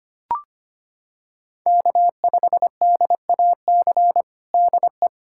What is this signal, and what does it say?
Morse code, heard as a keyed tone, about 25 wpm, 700 Hz. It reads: K5DAC DE
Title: Practice QSO - Exchange 4 - Ending - 25wpm